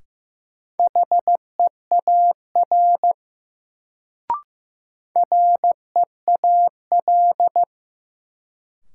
Morse code, heard as a keyed tone, about 16 wpm, 700 Hz. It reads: HEAR REAL
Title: Sentences from Top 300 Words - New Word Drill - 15wpm